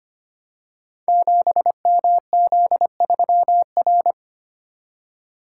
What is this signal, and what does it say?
Morse code, heard as a keyed tone, about 25 wpm, 700 Hz. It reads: 7MZ3R
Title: Code Groups - 5 Letter:Numbers - 25wpm